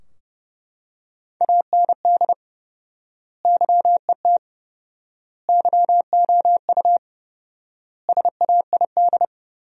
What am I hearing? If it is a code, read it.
Morse code, heard as a keyed tone, about 30 wpm, 700 Hz. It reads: AND YET YOU SAID
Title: 4x Word-Spacing - Sentences from Top 500 Words - 30wpm